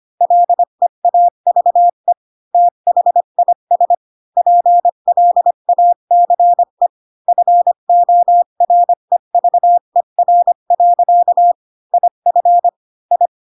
Morse code, heard as a keyed tone, about 25 wpm, 700 Hz. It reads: LEAVE THIS PLACE FOREVER. IF I
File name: Pinocchio / Pinocchio ch4 20 25 ik1hge